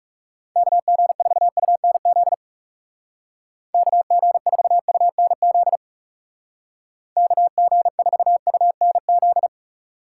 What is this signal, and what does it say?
Morse code, heard as a keyed tone, about 37 wpm, 700 Hz. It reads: KG4UNZ KG4UNZ KG4UNZ
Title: Speed Racing - USA Call Signs 30wpm